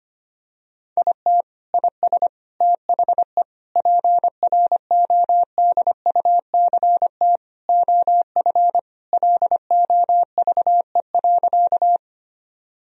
Morse code, heard as a keyed tone, about 25 wpm, 700 Hz. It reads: IT IS THE PRODUCT OF LOVE.
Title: Sentences from Top 1000 Words - Review Part 2 - 25wpm